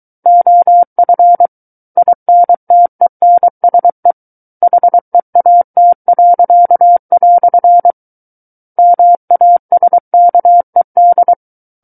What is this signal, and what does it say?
Morse code, heard as a keyed tone, about 23 wpm, 700 Hz. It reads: OF INTENSE HEAT." MASKED
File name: Book One / 14-Chapter 14 - 23 WPM